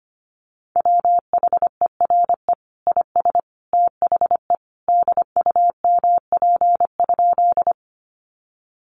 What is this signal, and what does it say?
Morse code, heard as a keyed tone, about 25 wpm, 700 Hz. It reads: WHERE IS THE DUMP?
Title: Sentences from Top 1300 Words 25wpm